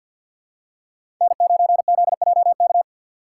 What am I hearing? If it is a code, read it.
Morse code, heard as a keyed tone, about 50 wpm, 700 Hz. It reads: N9ZJK
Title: USA Call Signs - 5 Characters - 4x Character Spacing - 50wpm